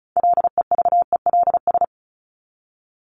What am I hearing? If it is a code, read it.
Morse code, heard as a keyed tone, about 34 wpm, 700 Hz. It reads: LEVELS
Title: ES:S Suffix - Top 100 Words with Base Words 35wpm